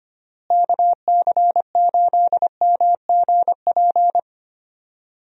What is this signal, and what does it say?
Morse code, heard as a keyed tone, about 25 wpm, 700 Hz. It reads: KC8MGP
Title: USA Call Signs - 6 Characters - Repeated 2x - 25wpm